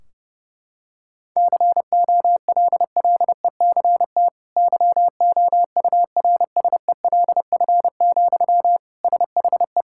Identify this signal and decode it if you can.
Morse code, heard as a keyed tone, about 32 wpm, 700 Hz. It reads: COLLECT YOURSELF, SHE
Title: Sentences from Top 1600 Words 30wpm